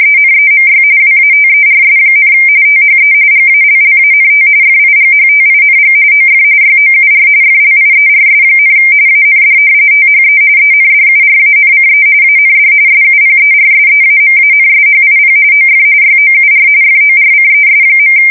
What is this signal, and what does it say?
RTTY, about 45.45 baud. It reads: ENEMY WATERS, THE GALLANT FIGHTING OF YOUR FLIERS, AND THE ENDURANCE AND SUPERSEAMANSHIP OF YOUR FORCES.
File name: fsk170 / battleshipnj